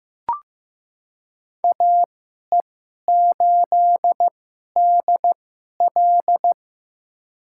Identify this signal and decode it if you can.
Morse code, heard as a keyed tone, about 15 wpm, spaced as for 11 wpm, 700 Hz. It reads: AE8DL
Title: USA Call Signs - 5 Characters - 2x Character Spacing - 15wpm